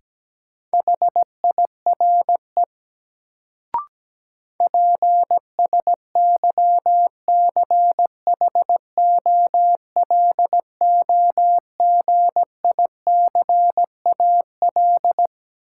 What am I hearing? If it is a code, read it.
Morse code, heard as a keyed tone, about 17 wpm, 700 Hz. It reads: HIRE PSYCHOLOGICAL
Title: Sentences from Top 1400 Words - New Word Drill - 17wpm